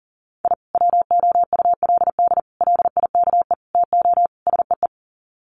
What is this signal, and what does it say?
Morse code, heard as a keyed tone, about 40 wpm, 700 Hz. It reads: I WOULD LIKE TO SEE
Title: Sets of 5 Words 40wpm